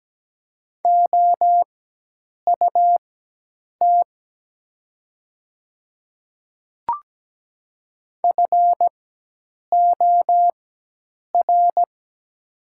Morse code, heard as a keyed tone, about 17 wpm, spaced as for 8 wpm, 690 Hz. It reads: OUT FOR
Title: 4x Character Spacing - Top 300 Words - 17wpm